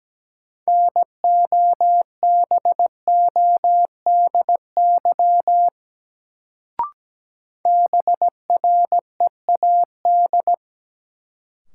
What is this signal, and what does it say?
Morse code, heard as a keyed tone, about 17 wpm, 700 Hz. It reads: NOBODY BREAD
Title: Sentences from Top 1600 Words - New Word Drill Alt - 17wpm